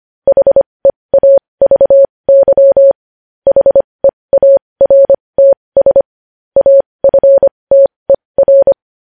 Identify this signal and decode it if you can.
Morse code, heard as a keyed tone, about 25 wpm, spaced as for 20 wpm, 550 Hz. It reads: HEAVY HEARTS AFTER
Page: News Headlines in Morse Code at 20 WPM